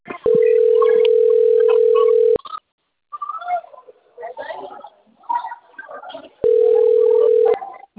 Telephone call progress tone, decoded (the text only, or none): ringback tone